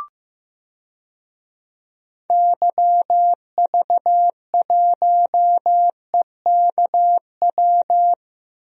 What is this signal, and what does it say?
Morse code, heard as a keyed tone, about 15 wpm, 700 Hz. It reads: YV1EKW
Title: DXCC-100 Call Signs - 6 Characters - Repeated 3x - 15wpm